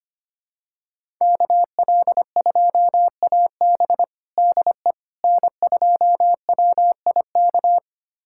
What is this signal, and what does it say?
Morse code, heard as a keyed tone, about 25 wpm, 700 Hz. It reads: KL2AB DE N2WIK
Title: USA Callsign DE Callsign - 5 Characters - 25wpm